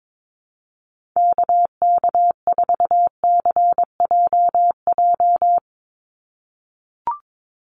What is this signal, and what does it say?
Morse code, heard as a keyed tone, about 22 wpm, 700 Hz. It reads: KK4CJJ
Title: USA Call Signs 22wpm